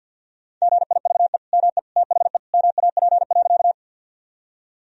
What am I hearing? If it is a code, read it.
Morse code, heard as a keyed tone, about 52 wpm, 700 Hz. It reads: GIVE ME THE MAP.
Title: Sentences from Top 1000 Words - Review Part 2 - 50wpm